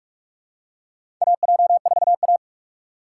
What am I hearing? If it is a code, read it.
Morse code, heard as a keyed tone, about 46 wpm, 700 Hz. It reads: AJ4A
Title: USA Call Signs - 4 Characters - 4x Character Spacing - 45wpm